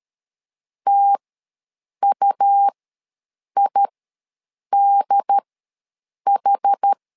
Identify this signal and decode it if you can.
Morse code, heard as a keyed tone, about 13 wpm, spaced as for 7 wpm, 800 Hz. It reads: TUIDH
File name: LEZIONI / 2 UHDN / 2 UHD ws 12wpm 1722